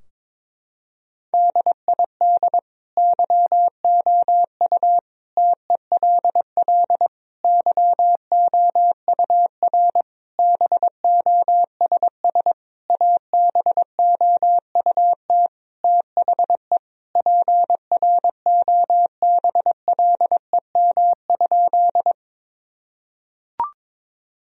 Morse code, heard as a keyed tone, about 22 wpm, 700 Hz. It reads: DID YOU TELL YOUR BOSS ABOUT THE PROBLEM?
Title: Sentences from Top 1000 Words - Review Part 4 - 22wpm